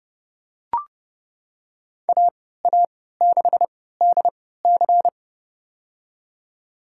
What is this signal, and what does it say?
Morse code, heard as a keyed tone, about 30 wpm, spaced as for 17 wpm, 700 Hz. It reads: AA6DC
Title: USA Call Signs - 5 Characters - 3x Character Spacing - 30wpm